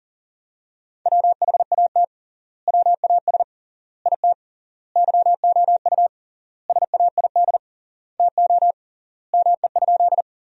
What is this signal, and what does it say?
Morse code, heard as a keyed tone, about 37 wpm, 700 Hz. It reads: WHAT WAS IT YOU SAID TO ME?
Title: 3x Word-Spacing - Sentences from Top 200 Words - 40wpm